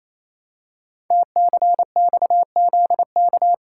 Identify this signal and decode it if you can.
Morse code, heard as a keyed tone, about 27 wpm, 700 Hz. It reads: TCXZK
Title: Code Groups - 5 Letters - 28wpm